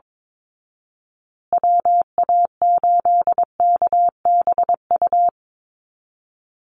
Morse code, heard as a keyed tone, about 22 wpm, 700 Hz. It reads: WA8KBU
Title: Call Signs Repeated 3x - 22wpm